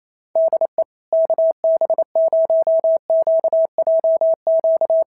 Morse code, heard as a keyed tone, about 28 wpm, 650 Hz. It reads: DE KB0QJQ
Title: SOTA QSO Practice 28wpm